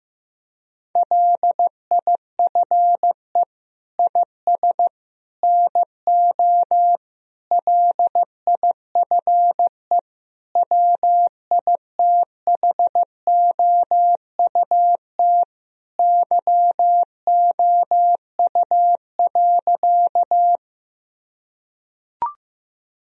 Morse code, heard as a keyed tone, about 15 wpm, 700 Hz. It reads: LIFE IS NO LIFE WITHOUT YOU.
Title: Sentences from Top 1000 Words - Review Part 3 - 15wpm